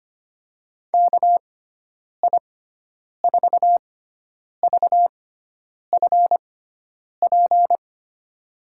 Morse code, heard as a keyed tone, about 25 wpm, spaced as for 9 wpm, 700 Hz. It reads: KI4VFP
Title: USA Call Signs - 6x Character Spacing - 25wpm